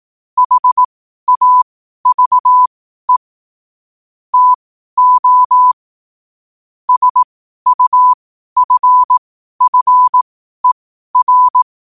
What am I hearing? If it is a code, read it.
Morse code, heard as a keyed tone, about 18 wpm, spaced as for 12 wpm, 1000 Hz. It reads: HAVE TO SUFFER
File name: qotdfiles / 12wpm